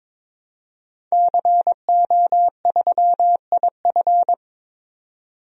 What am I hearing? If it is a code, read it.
Morse code, heard as a keyed tone, about 22 wpm, 700 Hz. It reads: CO3IF
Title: DXCC-100 Call Signs - 5 Characters - 22wpm